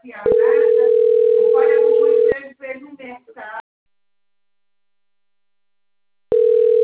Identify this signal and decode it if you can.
Ringback tone (North America), 440 + 480 Hz, with speech.
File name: rg-1600-unknown- - - .29879